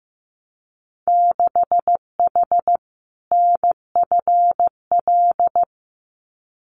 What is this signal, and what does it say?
Morse code, heard as a keyed tone, about 15 wpm, 700 Hz. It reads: 6H NFL
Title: Winter Field Day Exchanges Encore 15wpm